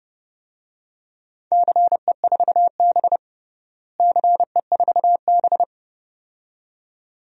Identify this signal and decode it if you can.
Morse code, heard as a keyed tone, about 30 wpm, 700 Hz. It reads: CE4B CE4B
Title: DXCC-100 Call Signs - 4 Characters - Repeated 2x - 30wpm